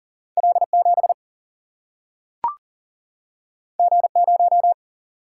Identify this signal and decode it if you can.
Morse code, heard as a keyed tone, about 40 wpm, 700 Hz. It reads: L 7 G 0
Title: Single-Letter Single-Number 40wpm